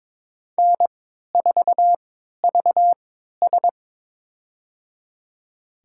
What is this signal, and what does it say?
Morse code, heard as a keyed tone, about 22 wpm, spaced as for 13 wpm, 700 Hz. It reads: N4VS
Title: USA Call Signs - 4 Characters - 3x Character Spacing - 22wpm